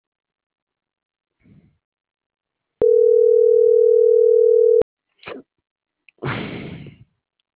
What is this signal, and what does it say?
Ringback tone (North America), 440 + 480 Hz.